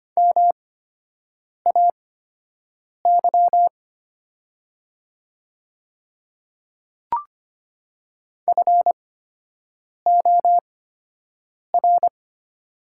Morse code, heard as a keyed tone, about 25 wpm, spaced as for 7 wpm, 700 Hz. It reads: MAY FOR
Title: 8x Character Spacing - Top 200 Words - 25wpm